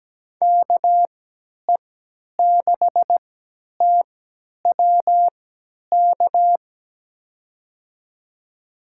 Morse code, heard as a keyed tone, about 17 wpm, spaced as for 10 wpm, 700 Hz. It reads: KE6TWK
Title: USA Call Signs - 6 Characters - 3x Character Spacing - 17wpm